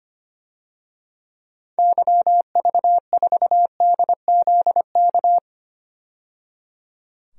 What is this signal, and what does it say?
Morse code, heard as a keyed tone, about 25 wpm, 700 Hz. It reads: YV4DZK
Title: DXCC-100 Call Signs - 6 Characters - 25wpm